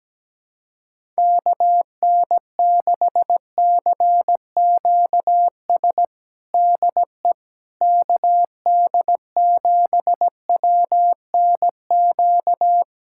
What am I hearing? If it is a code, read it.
Morse code, heard as a keyed tone, about 17 wpm, 700 Hz. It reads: KN6CQS DE KD7WNQ
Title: USA Callsign DE Callsign - 6 Characters - 17wpm